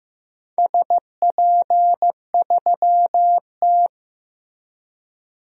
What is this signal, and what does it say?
Morse code, heard as a keyed tone, about 15 wpm, 700 Hz. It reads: SP3T
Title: DXCC-100 Call Signs - 4 Characters - Repeated 3x - 15wpm